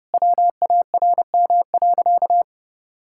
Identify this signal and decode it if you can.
Morse code, heard as a keyed tone, about 28 wpm, 700 Hz. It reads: WARM.
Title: Sentences from Top 1000 Words - Review Part 5 - 30wpm